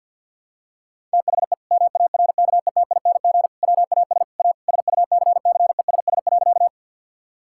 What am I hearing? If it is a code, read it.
Morse code, heard as a keyed tone, about 47 wpm, 700 Hz. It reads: THE MARKETING WAS A SUCCESS.
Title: Sentences from Top 1000 Words - Review Part 5 - 50wpm